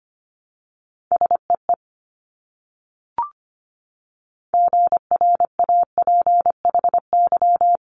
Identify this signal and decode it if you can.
Morse code, heard as a keyed tone, about 25 wpm, 700 Hz. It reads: SEE GRAPHY
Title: Multi-letter Phonemes Repeated 3x 25wpm